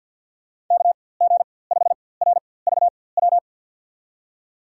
Morse code, heard as a keyed tone, about 53 wpm, spaced as for 24 wpm, 700 Hz. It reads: KG5RVW
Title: USA Call Signs - 6 Characters - 4x Character Spacing - 50wpm